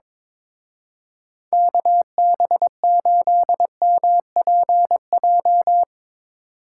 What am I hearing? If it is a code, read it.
Morse code, heard as a keyed tone, about 24 wpm, 700 Hz. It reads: KB8MPJ
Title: Speed Racing - USA 2x3 Call Signs 22wpm